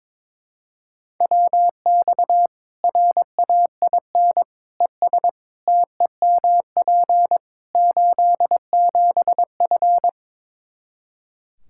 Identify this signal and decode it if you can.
Morse code, heard as a keyed tone, about 22 wpm, 700 Hz. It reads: WX RAIN ES TEMP 87F
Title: Practice QSO - Exchange 2 - Info - 22wpm